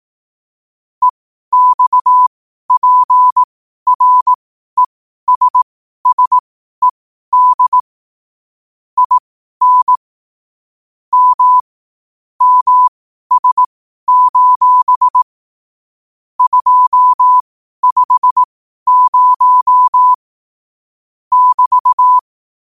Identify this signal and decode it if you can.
Morse code, heard as a keyed tone, about 18 wpm, spaced as for 12 wpm, 1000 Hz. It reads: EXPRESSED IN M MS: 250 <BT>
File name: qotdfiles / 12wpm